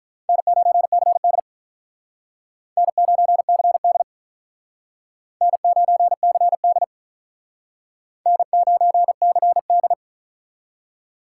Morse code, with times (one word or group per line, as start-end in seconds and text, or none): 0.3-1.4 N9CD
2.8-4.0 N9CD
5.4-6.8 N9CD
8.3-9.9 N9CD